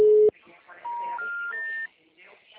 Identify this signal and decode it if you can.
Special information tone, with speech.